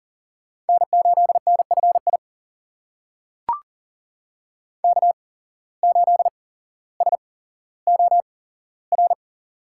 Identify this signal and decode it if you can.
Morse code, heard as a keyed tone, about 40 wpm, spaced as for 17 wpm, 700 Hz. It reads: N8NFI K8SOR
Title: USA Call Signs - 5 Characters - 8x Character Spacing - 40wpm